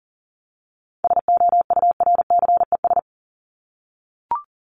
Morse code, heard as a keyed tone, about 40 wpm, 700 Hz. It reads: SOURCES
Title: ES:S Suffix - Top 100 Words with Base Words 40wpm